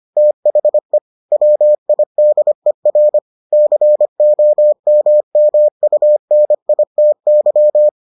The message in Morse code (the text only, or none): THE WIDER COMMUNITY